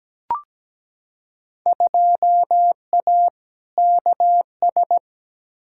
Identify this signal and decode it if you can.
Morse code, heard as a keyed tone, about 17 wpm, 700 Hz. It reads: 2A KS
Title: Field Day Exchanges - 17wpm